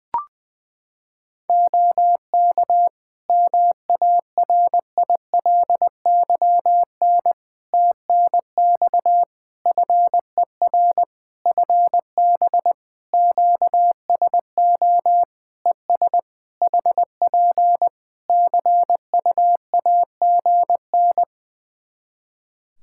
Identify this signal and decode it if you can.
Morse code, heard as a keyed tone, about 20 wpm, 700 Hz. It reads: OK MARILYN TNX FER FB QSO ES HP CUAGN
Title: Practice QSO - Exchange 4 - Ending - 20wpm